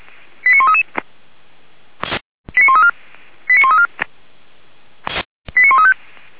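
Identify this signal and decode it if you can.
Five-tone selective call (ZVEI), heard as quick runs of five tones, tones of about 70 ms. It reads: 89133 89134 89134 89135
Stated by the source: ZVEI five-tone selective call received on an SDR by the Feuerwehr Orlamünde (fire brigade).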